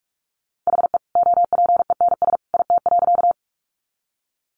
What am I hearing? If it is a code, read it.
Morse code, heard as a keyed tone, about 44 wpm, 700 Hz. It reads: HE OPENS IT.
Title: Additional Easy Phrases 45wpm